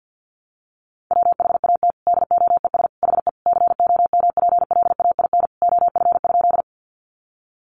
Morse code, heard as a keyed tone, about 50 wpm, 700 Hz. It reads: WHAT DOES HE COMPLAIN OF?
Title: Sentences from Top 1300 Words 50wpm